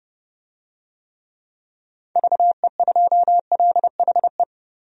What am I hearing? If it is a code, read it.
Morse code, heard as a keyed tone, about 30 wpm, 700 Hz. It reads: VE2LHE
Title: DXCC-100 Call Signs - 6 Characters - 30wpm